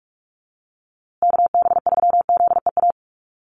Morse code, heard as a keyed tone, about 44 wpm, 700 Hz. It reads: KB3ZEA
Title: USA Call Signs Repeated 3x 45wpm